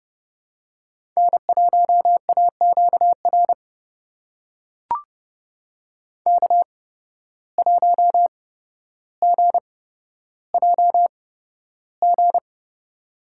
Morse code, heard as a keyed tone, about 30 wpm, spaced as for 13 wpm, 700 Hz. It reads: N1AQR K1GJG
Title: USA Call Signs - 5 Characters - 8x Character Spacing - 30wpm